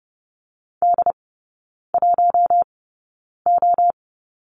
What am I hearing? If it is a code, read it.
Morse code, heard as a keyed tone, about 30 wpm, spaced as for 9 wpm, 700 Hz. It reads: D1O
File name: Lesson 16e - Letter c - Three Characters 30wpm